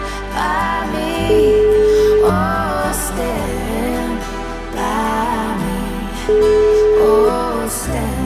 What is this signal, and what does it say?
Ringback tone (Europe, 425 Hz), with noise.